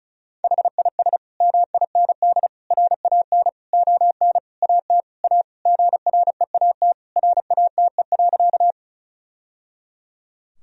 Morse code, heard as a keyed tone, about 36 wpm, 700 Hz. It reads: HIS MIND RAN ON AT A GREAT RATE.
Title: Sentences from Top 1000 Words - Review Part 4 - 35wpm